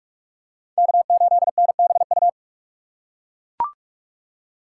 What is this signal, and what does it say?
Morse code, heard as a keyed tone, about 40 wpm, 700 Hz. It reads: K8NBU